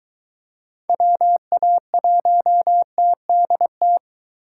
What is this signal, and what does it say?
Morse code, heard as a keyed tone, about 23 wpm, 700 Hz. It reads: WA1TDT